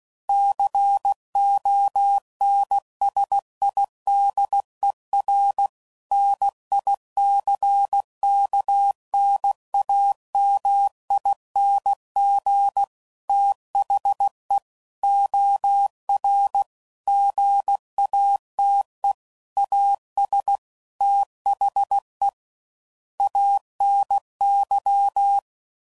Morse code, heard as a keyed tone, about 16 wpm, 800 Hz. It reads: CONSIDER NICKNAMING THE OR GATE AS THE ANY
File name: CW Code Groups @15WPM / Technical Subjects / OR Gate